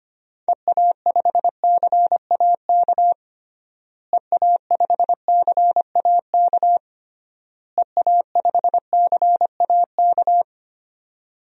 Morse code, heard as a keyed tone, about 25 wpm, 700 Hz. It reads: EA5CAK EA5CAK EA5CAK
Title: DXCC-100 Call Signs - 6 Characters - Repeated 3x - 25wpm